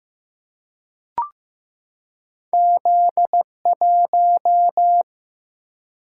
Morse code, heard as a keyed tone, about 15 wpm, 700 Hz. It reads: Z1
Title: Single-Letter Single-Number 15wpm